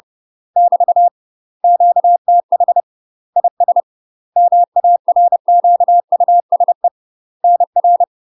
Morse code, heard as a keyed tone, about 30 wpm, 700 Hz. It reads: <BT> QTH IS MARQUSE NR
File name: UK MASTER QSOs TRAINING / 30WPM / UK MASTER QSOs 0036